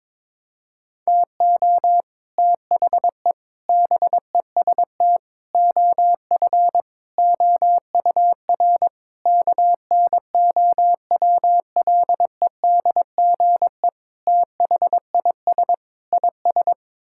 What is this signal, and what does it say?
Morse code, heard as a keyed tone, about 21 wpm, 700 Hz. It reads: TO THE BEST OF OUR KNOWLEDGE THIS IS
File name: practice / 22wpm / Sets of 8 Words - 22wpm